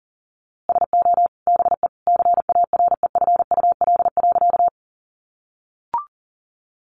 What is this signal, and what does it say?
Morse code, heard as a keyed tone, about 40 wpm, 700 Hz. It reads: SO BE CAREFUL.
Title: A Sound Of Thunder 40wpm